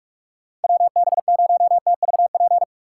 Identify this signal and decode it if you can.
Morse code, heard as a keyed tone, about 47 wpm, 700 Hz. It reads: WB0TVP